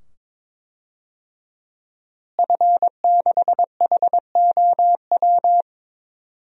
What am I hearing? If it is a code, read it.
Morse code, heard as a keyed tone, about 22 wpm, 700 Hz. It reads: F6HOW
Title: DXCC-100 Call Signs - 5 Characters - Repeated 2x - 22wpm